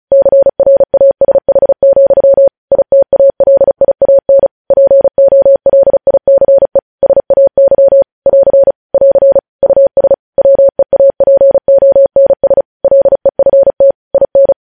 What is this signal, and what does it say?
Morse code, heard as a keyed tone, about 36 wpm, 550 Hz. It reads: CRASH, ITALIAN POLICE SAY <AR> <AR> US WEAPONS LEFT IN
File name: podcast / News headlines-1x-35x35